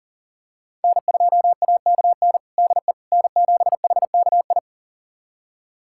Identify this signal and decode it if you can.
Morse code, heard as a keyed tone, about 37 wpm, 700 Hz. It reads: N2AKN DE N7HKI
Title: USA Callsign DE Callsign - 5 Characters - 40wpm